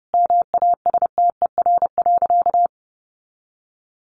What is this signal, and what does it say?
Morse code, heard as a keyed tone, about 30 wpm, 700 Hz. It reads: MASTER.
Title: Sentences from Top 1300 Words 30wpm